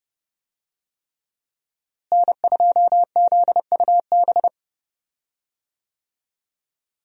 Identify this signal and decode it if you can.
Morse code, heard as a keyed tone, about 30 wpm, 700 Hz. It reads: N2ZUB
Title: USA Call Signs - 5 Characters - Repeated 3x - 30wpm